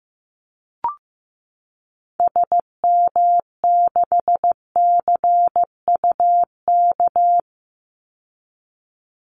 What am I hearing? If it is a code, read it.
Morse code, heard as a keyed tone, about 15 wpm, 700 Hz. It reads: SM6CUK
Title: Call Signs Repeated 2x - 15wpm